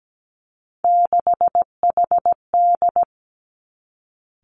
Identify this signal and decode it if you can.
Morse code, heard as a keyed tone, about 17 wpm, 700 Hz. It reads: 6HD
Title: Three Challenging Letter-Number Combinations 17wpm